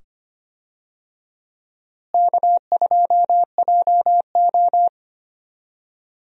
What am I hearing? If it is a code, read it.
Morse code, heard as a keyed tone, about 25 wpm, 700 Hz. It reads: K2JO
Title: USA Call Signs - 4 Characters - Repeated 2x - 25wpm